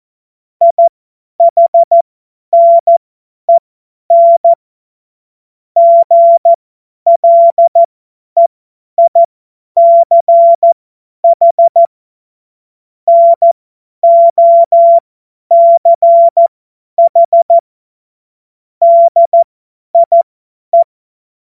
Morse code, heard as a keyed tone, about 14 wpm, spaced as for 10 wpm, 680 Hz. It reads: IHNEN GLEICH NOCH DIE
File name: Morsekurs / Witze / Witz 04. Juli 2023 50 F70